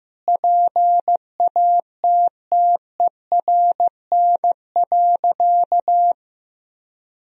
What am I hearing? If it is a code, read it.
Morse code, heard as a keyed tone, about 15 wpm, 700 Hz. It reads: PATTERN.
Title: Sentences from Top 1000 Words - Review Part 2 - 15wpm